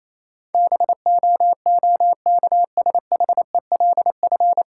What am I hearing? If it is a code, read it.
Morse code, heard as a keyed tone, about 28 wpm, 700 Hz. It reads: BOOKSHELF
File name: Easy Words - Speed-Racing with Hints - 22wpm